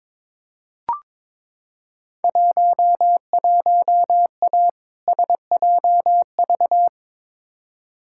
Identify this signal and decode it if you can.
Morse code, heard as a keyed tone, about 22 wpm, 700 Hz. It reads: 11A SJV
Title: Field Day Exchanges Encore - 22wpm